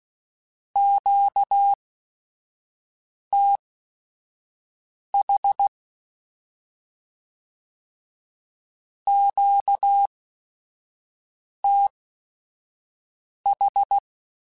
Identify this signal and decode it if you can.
Morse code, heard as a keyed tone, about 16 wpm, spaced as for 5 wpm, 800 Hz. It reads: QTH QTH
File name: Q Calls / Q Calls-mp3s / Q Calls Twice @5